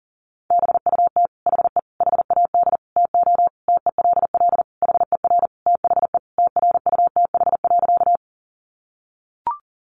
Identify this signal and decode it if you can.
Morse code, heard as a keyed tone, about 40 wpm, 700 Hz. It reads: BUT HE HAD TO TELL HER THE TRUTH.
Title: Sentences from Top 800 Words 40wpm